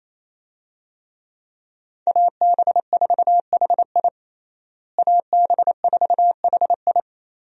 Morse code, heard as a keyed tone, about 28 wpm, 700 Hz. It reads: AB4HI AB4HI
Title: USA Call Signs - 5 Characters - Repeated 3x - 28wpm